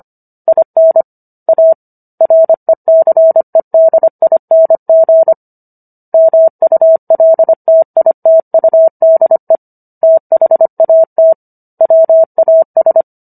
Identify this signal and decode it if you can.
Morse code, heard as a keyed tone, about 25 wpm, 650 Hz. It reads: IN A RECEDING MULTITUDE THAT WAS